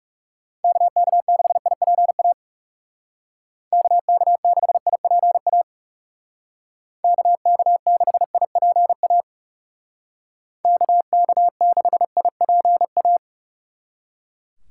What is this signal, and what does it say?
Morse code, heard as a keyed tone, about 38 wpm, 700 Hz. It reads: KK6IPA KK6IPA KK6IPA KK6IPA
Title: Speed Racing - USA 2x3 Call Signs 30wpm